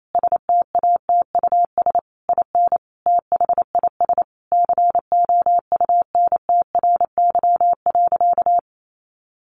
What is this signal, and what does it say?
Morse code, heard as a keyed tone, about 28 wpm, 700 Hz. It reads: STATUS IN THIS COUNTRY.
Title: Sentences from Top 1400 Words 28wpm